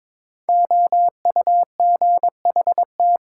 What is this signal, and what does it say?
Morse code, heard as a keyed tone, about 22 wpm, 700 Hz. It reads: OUGHT
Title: Sets of 5 Words Encore - New Word Drill - 22wpm